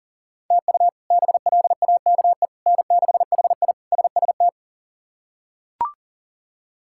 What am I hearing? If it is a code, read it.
Morse code, heard as a keyed tone, about 40 wpm, 700 Hz. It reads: TU BLAKE N6HI SST
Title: K1USN SST QSO Practice - 40wpm